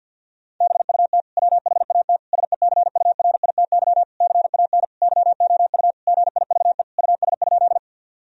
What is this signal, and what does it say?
Morse code, heard as a keyed tone, about 50 wpm, 700 Hz. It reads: BUT WHAT SECURITY CAN YOU GIVE US?
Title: Sentences from Top 1000 Words - Review Part 4 - 50wpm